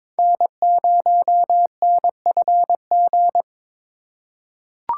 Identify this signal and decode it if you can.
Morse code, heard as a keyed tone, about 23 wpm, 700 Hz. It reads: N0NFG
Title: USA Call Signs - 5 Characters - 4x Character Spacing - 22wpm